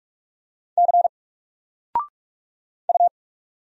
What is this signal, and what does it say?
Morse code, heard as a keyed tone, about 43 wpm, 700 Hz. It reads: C U
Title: Single Character (Letter:Number:Punctuation:Prosign) Alt 45wpm